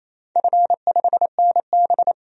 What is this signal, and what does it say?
Morse code, heard as a keyed tone, about 28 wpm, 690 Hz. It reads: F5NB
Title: DXCC-100 Call Signs - 4 Characters - 28wpm